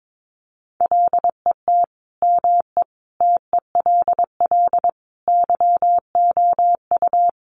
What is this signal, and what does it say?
Morse code, heard as a keyed tone, about 22 wpm, 700 Hz. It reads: LET ME TELL YOU
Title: Sets of 5 Words Encore 22wpm